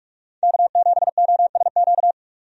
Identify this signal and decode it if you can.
Morse code, heard as a keyed tone, about 47 wpm, 700 Hz. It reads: K7OSQ
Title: USA Call Signs - 5 Characters - 45wpm